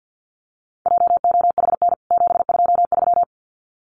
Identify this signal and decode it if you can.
Morse code, heard as a keyed tone, about 48 wpm, 700 Hz. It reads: JOHN 723
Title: CWT Exchange 50wpm